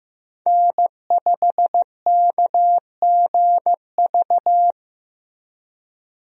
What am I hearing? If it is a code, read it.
Morse code, heard as a keyed tone, about 15 wpm, 700 Hz. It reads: N5KGV
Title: USA Call Signs - 5 Characters - Repeated 2x - 15wpm